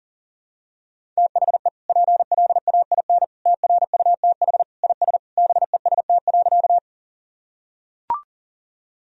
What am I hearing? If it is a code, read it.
Morse code, heard as a keyed tone, about 40 wpm, 700 Hz. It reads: THE PLAIN TRUTH IS BEST.
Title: Sentences from Top 1000 Words - Review Part 2 - 40wpm